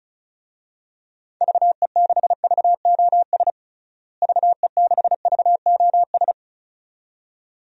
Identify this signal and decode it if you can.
Morse code, heard as a keyed tone, about 36 wpm, 700 Hz. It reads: VE6VOS VE6VOS
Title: DXCC-100 Call Signs - 6 Characters - Repeated 2x - 35wpm